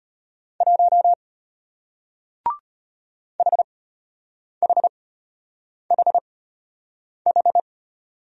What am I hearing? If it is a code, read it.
Morse code, heard as a keyed tone, about 34 wpm, spaced as for 8 wpm, 700 Hz. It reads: 1 HHHH